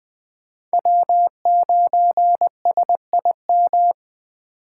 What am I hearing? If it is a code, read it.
Morse code, heard as a keyed tone, about 21 wpm, 700 Hz. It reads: W9SIM